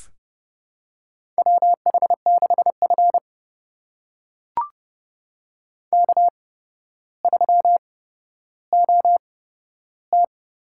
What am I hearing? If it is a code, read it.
Morse code, heard as a keyed tone, about 30 wpm, spaced as for 13 wpm, 700 Hz. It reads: WH6F K3OT
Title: USA Call Signs - 8x Character Spacing - 30wpm